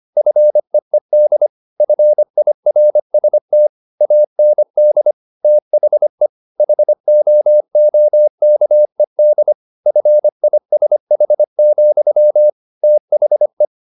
Morse code, heard as a keyed tone, about 25 wpm, 600 Hz. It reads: FEED FIRST AND THE HOOKED FISH, THE
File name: the old man and the sea part 1 25wpm